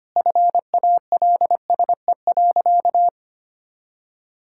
Morse code, heard as a keyed tone, about 25 wpm, 700 Hz. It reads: FALSE.
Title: Sentences from Top 1600 Words 25wpm